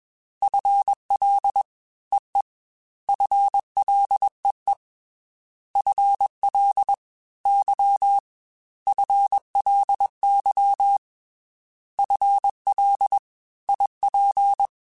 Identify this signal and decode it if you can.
Morse code, heard as a keyed tone, about 21 wpm, 800 Hz. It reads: FL EE FLEE FL Y FLY FL IP
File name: FL / FL mp3s / Fast FL @20